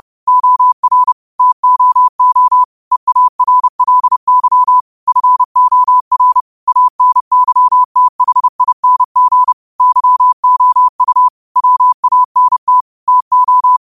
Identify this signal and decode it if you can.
Morse code, heard as a keyed tone, about 30 wpm, 1000 Hz. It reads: OR TOO EARLY FOR ANYTHING YOU WANT TO
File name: qotdfiles / 30wpm